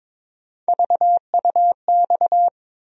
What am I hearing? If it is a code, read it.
Morse code, heard as a keyed tone, about 22 wpm, 700 Hz. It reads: VUX
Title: USA Call Sign Postfixes - 3 Letters 22wpm